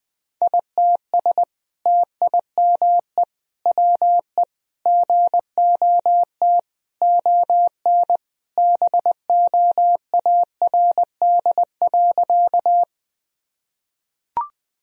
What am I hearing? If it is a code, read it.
Morse code, heard as a keyed tone, about 20 wpm, 700 Hz. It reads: ITS TIME WE GOT ON BOARD.
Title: Sentences from Top 1000 Words - Review Part 4 - 20wpm